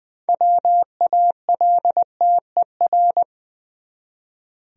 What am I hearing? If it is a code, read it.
Morse code, heard as a keyed tone, about 20 wpm, 700 Hz. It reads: WALTER
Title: USA Common Names - 20wpm